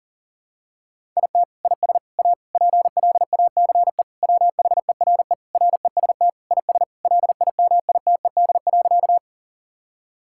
Morse code, heard as a keyed tone, about 40 wpm, 700 Hz. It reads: IT IS A PLACE WHERE REST IS LIMITED.
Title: Sentences from Top 1000 Words - Review Part 6 - 40wpm